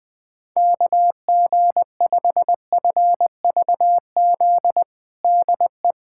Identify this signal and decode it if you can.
Morse code, heard as a keyed tone, about 20 wpm, 700 Hz. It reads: KG5FVZ DE
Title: USA Callsign DE Callsign - 6 Characters - 20wpm